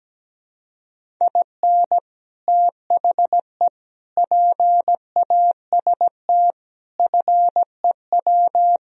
Morse code, heard as a keyed tone, about 18 wpm, 700 Hz. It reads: IN THE PAST FEW
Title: Speed-Racing - Sets of 5 Words 17wpm